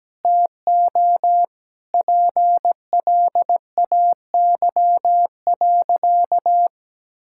Morse code, heard as a keyed tone, about 17 wpm, 700 Hz. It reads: TO PLAY.
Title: Sentences from Top 1000 Words - Review Part 6 - 17wpm